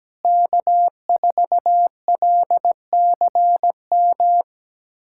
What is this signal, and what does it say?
Morse code, heard as a keyed tone, about 17 wpm, 700 Hz. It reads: K4LCM
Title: USA Callsign DE Callsign - 5 Characters - 17wpm